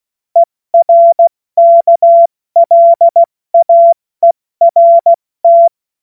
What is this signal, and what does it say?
Morse code, heard as a keyed tone, about 16 wpm, 680 Hz. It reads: ERKLAERT
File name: Morsekurs / Witze / Witz 30. Dezember 2025 70 F80